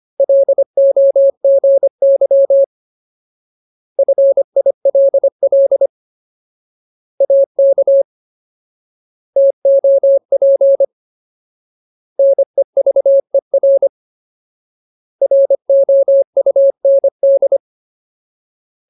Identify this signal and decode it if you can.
Morse code, heard as a keyed tone, about 25 wpm, 550 Hz. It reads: LOGY FILL AK TOP NEVER ROUND
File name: Ultimate Word List / 25WPM Ultimate